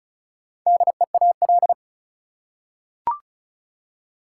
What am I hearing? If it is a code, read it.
Morse code, heard as a keyed tone, about 34 wpm, 700 Hz. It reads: DEAL
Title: Four Letter Words - 35wpm